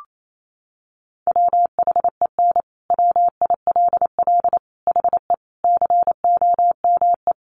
Morse code, heard as a keyed tone, about 28 wpm, 700 Hz. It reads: WHEN WILL HE COME
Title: Sentences from Top 200 Words 28wpm